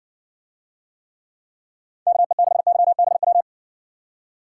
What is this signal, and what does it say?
Morse code, heard as a keyed tone, about 53 wpm, 700 Hz. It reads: KE6YBW